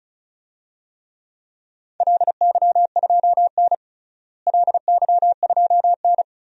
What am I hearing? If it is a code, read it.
Morse code, heard as a keyed tone, about 34 wpm, 700 Hz. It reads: LY2N LY2N
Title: DXCC-100 Call Signs - 4 Characters - Repeated 2x - 35wpm